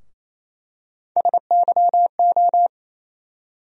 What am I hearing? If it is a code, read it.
Morse code, heard as a keyed tone, about 28 wpm, 700 Hz. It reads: SYO
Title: USA Call Sign Postfixes - 3 Letters 28wpm